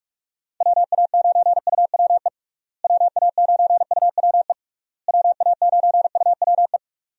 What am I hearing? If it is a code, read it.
Morse code, heard as a keyed tone, about 46 wpm, 700 Hz. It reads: WA9UWE WA9UWE WA9UWE
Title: USA Call Signs - 6 Characters - Repeated 3x - 45wpm